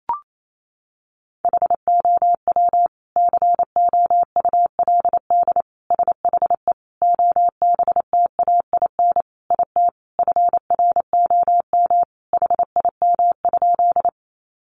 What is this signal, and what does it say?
Morse code, heard as a keyed tone, about 28 wpm, 700 Hz. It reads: HOW COULD SHE OBTAIN IT FROM HIM?
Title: Sentences from Top 1300 Words 28wpm